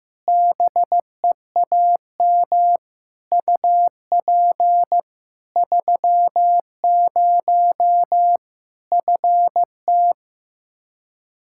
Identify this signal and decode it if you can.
Morse code, heard as a keyed tone, about 16 wpm, 700 Hz. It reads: BEAM UP 30 FT
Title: K1USN SST QSO Practice 15wpm